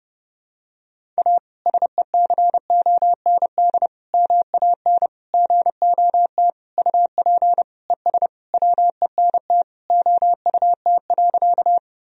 Morse code, heard as a keyed tone, about 29 wpm, 700 Hz. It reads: A SECOND MAN GOT UP ES WENT OUT.
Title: Sentences from Top 300 Words and Abbreviations 30wpm